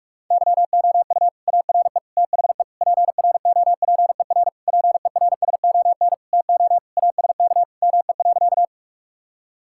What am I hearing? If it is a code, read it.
Morse code, heard as a keyed tone, about 42 wpm, 690 Hz. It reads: YOU ARE THE PROPER PERSON TO ASK ME.
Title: Sentences from Top 1000 Words - Review Part 5 - 45wpm